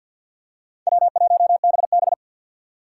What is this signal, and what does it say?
Morse code, heard as a keyed tone, about 47 wpm, 700 Hz. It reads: W1BB
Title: USA Call Signs - 4 Characters - 3x Character Spacing - 50wpm